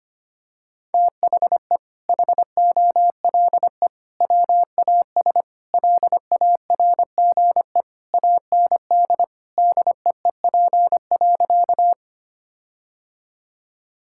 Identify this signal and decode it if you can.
Morse code, heard as a keyed tone, about 25 wpm, 700 Hz. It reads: THE HOLE WAS LARGE AND DEEP.
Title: Sentences from Top 1600 Words 25wpm